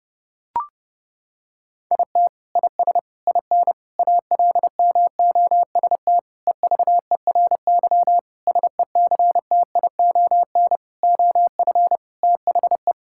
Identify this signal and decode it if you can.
Morse code, heard as a keyed tone, about 30 wpm, 700 Hz. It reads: IT IS IN ALMOST EVERY SECTION OF THE
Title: Sentences from Top 1000 Words - Review Part 5 - 30wpm